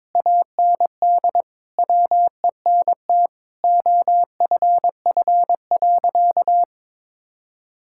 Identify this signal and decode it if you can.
Morse code, heard as a keyed tone, about 22 wpm, 700 Hz. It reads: AND WENT OFF.
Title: Sentences from Top 1600 Words 22wpm